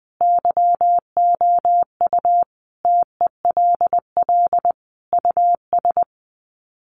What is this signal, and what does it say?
Morse code, heard as a keyed tone, about 20 wpm, 700 Hz. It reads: YOU TELL US
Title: Sets of 5 Words Encore 20wpm